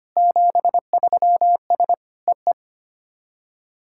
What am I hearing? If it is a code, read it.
Morse code, heard as a keyed tone, about 25 wpm, 700 Hz. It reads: 73S EE
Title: SOTA QSO Practice 25wpm